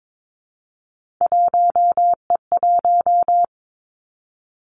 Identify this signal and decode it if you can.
Morse code, heard as a keyed tone, about 22 wpm, 700 Hz. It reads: 1E1
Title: Three Challenging Letter-Number Combinations 22wpm